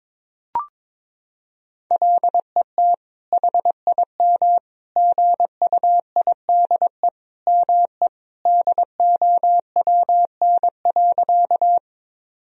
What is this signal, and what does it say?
Morse code, heard as a keyed tone, about 22 wpm, 700 Hz. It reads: LET HIM GUIDE ME DOWN.
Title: Sentences from Top 1000 Words - Review Part 3 - 22wpm